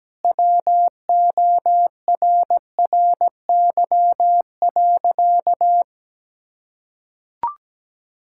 Morse code, heard as a keyed tone, about 17 wpm, 700 Hz. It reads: WORRY.
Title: Pangram Challenge v2 - 17wpm